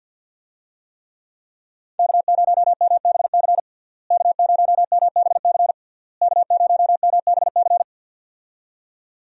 Morse code, heard as a keyed tone, about 52 wpm, 700 Hz. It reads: K0MBC K0MBC K0MBC
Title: USA Call Signs - 5 Characters - Repeated 3x - 50wpm